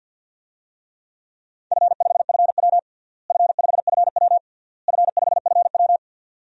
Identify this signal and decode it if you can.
Morse code, heard as a keyed tone, about 52 wpm, 700 Hz. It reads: F5FW F5FW F5FW
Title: DXCC-100 Call Signs - 4 Characters - Repeated 3x - 50wpm